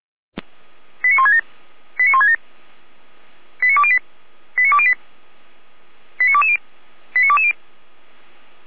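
Five-tone selective call (ZVEI), heard as quick runs of five tones, tones of about 70 ms. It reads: 89167 89167 89208 89208 89220 89220
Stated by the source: ZVEI five-tone selective call 89119 received by SDR.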